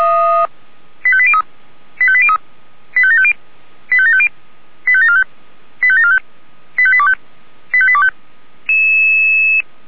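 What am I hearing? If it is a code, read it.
Five-tone selective call (ZVEI), heard as quick runs of five tones, tones of about 70 ms. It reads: 85803 85803 85750 85750 85745 85745 85825 85825
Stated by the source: ZVEI five-tone selective call received by SDR.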